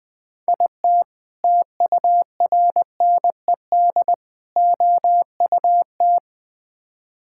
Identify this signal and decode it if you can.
Morse code, heard as a keyed tone, about 20 wpm, 700 Hz. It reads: IT TURNED OUT
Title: Build A Phrase - 5-Word Phrases - 20wpm